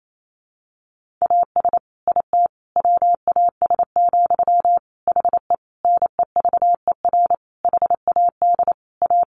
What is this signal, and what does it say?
Morse code, heard as a keyed tone, about 28 wpm, 700 Hz. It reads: AS IT WAS, HE NEVER HAD A
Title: Sentences from Top 1400 Words 28wpm